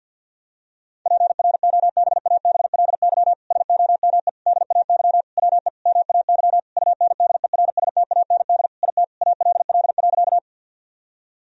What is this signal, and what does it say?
Morse code, heard as a keyed tone, about 51 wpm, 700 Hz. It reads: PROBABLY SOME DAY WE MAY UNDERSTAND IT ALL.
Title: Sentences from Top 1000 Words - Review Part 4 - 50wpm